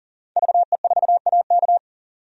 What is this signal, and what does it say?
Morse code, heard as a keyed tone, about 40 wpm, 700 Hz. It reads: VE4AK
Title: DXCC-100 Call Signs - 5 Characters - 40wpm